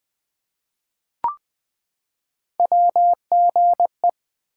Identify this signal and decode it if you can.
Morse code, heard as a keyed tone, about 20 wpm, 700 Hz. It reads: WGE
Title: Three Letter-Number Combinations - 20wpm